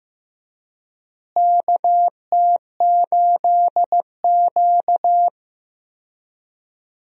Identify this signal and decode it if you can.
Morse code, heard as a keyed tone, about 15 wpm, 700 Hz. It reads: KT8Q
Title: USA Call Signs - 4 Characters - Repeated 2x - 15wpm